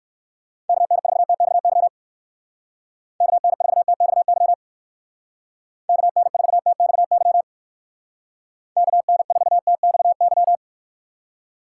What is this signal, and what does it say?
Morse code, heard as a keyed tone, about 56 wpm, 700 Hz. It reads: KN4TXY KN4TXY KN4TXY KN4TXY
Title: Speed Racing - USA 2x3 Call Signs 45wpm